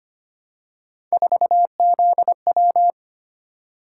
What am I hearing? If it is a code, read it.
Morse code, heard as a keyed tone, about 25 wpm, 700 Hz. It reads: 4ZW
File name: Three Letter-Number Combinations - 25wpm